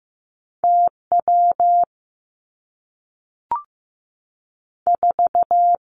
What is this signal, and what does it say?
Morse code, heard as a keyed tone, about 15 wpm, 700 Hz. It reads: T W 4
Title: Two Challenging Letter-Number Combinations 15wpm